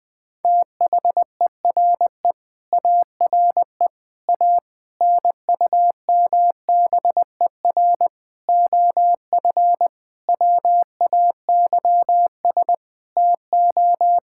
Morse code, heard as a keyed tone, about 20 wpm, 700 Hz. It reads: THERE ARE A NUMBER OF WAYS TO
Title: Sentences with 7-Word Phrases - 20wpm